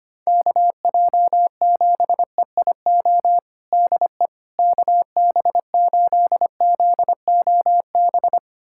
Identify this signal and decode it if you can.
Morse code, heard as a keyed tone, about 25 wpm, 700 Hz. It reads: KJ7EIO DE KB8ZOB
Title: USA Callsign DE Callsign 25wpm